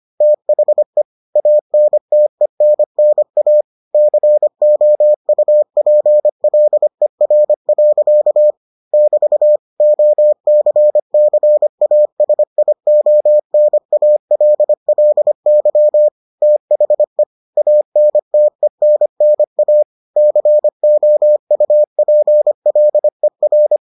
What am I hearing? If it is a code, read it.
Morse code, heard as a keyed tone, about 25 wpm, 600 Hz. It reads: THE ANTENNA COUPLER. <BT> OCCASIONALLY THE ANTENNA COUPLER